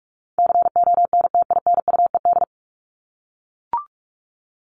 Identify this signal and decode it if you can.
Morse code, heard as a keyed tone, about 44 wpm, 700 Hz. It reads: CONTINUED
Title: ED Suffix - Top 100 Words with Base Words 45wpm